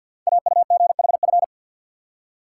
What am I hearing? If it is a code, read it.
Morse code, heard as a keyed tone, about 50 wpm, 690 Hz. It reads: AUGHF
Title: Code Groups - 5 Letters - 50wpm